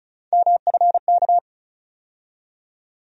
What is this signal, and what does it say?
Morse code, heard as a keyed tone, about 36 wpm, 700 Hz. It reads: MFK
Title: USA Call Sign Postfixes - 3 Letters 35wpm